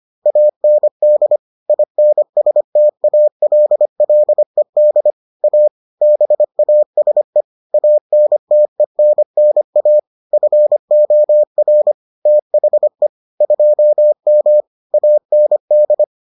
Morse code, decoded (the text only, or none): AND INSTALLED A BASE ANTENNA FOR THE 2M AND